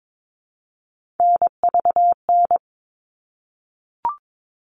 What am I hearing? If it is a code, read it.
Morse code, heard as a keyed tone, about 22 wpm, 700 Hz. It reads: NVN
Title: All Cut Numbers - 3-Digit 22wpm